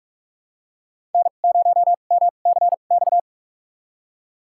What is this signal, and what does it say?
Morse code, heard as a keyed tone, about 42 wpm, spaced as for 31 wpm, 690 Hz. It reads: N0MCX
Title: USA Call Signs - 5 Characters - 2x Character Spacing - 45wpm